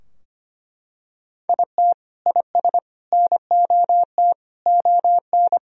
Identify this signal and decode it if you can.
Morse code, heard as a keyed tone, about 25 wpm, 700 Hz. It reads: IT IS NOT ON
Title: Sentences from Top 1000 Words - Review Part 2 - 25wpm